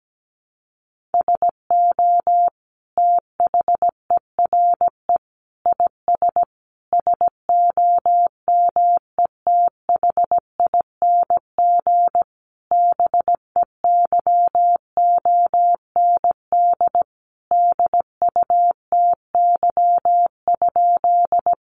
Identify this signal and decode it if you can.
Morse code, heard as a keyed tone, about 17 wpm, 700 Hz. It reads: SO THERE IS SOMETHING BEYOND DUTY?
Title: Sentences from Top 1500 Words 17wpm